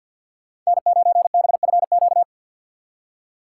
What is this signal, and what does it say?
Morse code, heard as a keyed tone, about 65 wpm, spaced as for 57 wpm, 700 Hz. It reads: N9BFQ
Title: Speed Racing - USA 1x3 Call Signs 50wpm